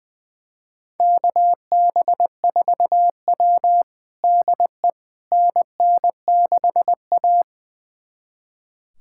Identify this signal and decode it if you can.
Morse code, heard as a keyed tone, about 20 wpm, 700 Hz. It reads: KB4W DE NN6A
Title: USA Callsign DE Callsign - 4 Characters - 20wpm